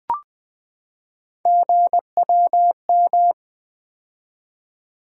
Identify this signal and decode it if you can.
Morse code, heard as a keyed tone, about 20 wpm, 700 Hz. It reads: GWM
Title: Three Letter-Number Combinations - 20wpm